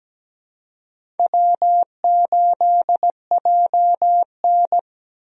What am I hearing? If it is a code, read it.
Morse code, heard as a keyed tone, about 17 wpm, 700 Hz. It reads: W8JN
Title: USA Call Signs - 4 Characters - Repeated 2x - 17wpm